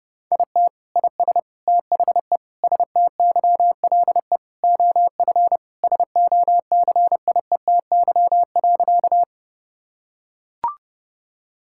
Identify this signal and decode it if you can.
Morse code, heard as a keyed tone, about 30 wpm, 700 Hz. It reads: IT IS THE STYLE OF SOCIETY.
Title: Sentences from Top 1000 Words - Review Part 5 - 30wpm